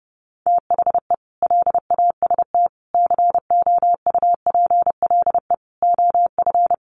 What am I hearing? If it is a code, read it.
Morse code, heard as a keyed tone, about 30 wpm, 700 Hz. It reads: THE LAST COUPLE OF
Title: Sets of 5 Words Encore2 30wpm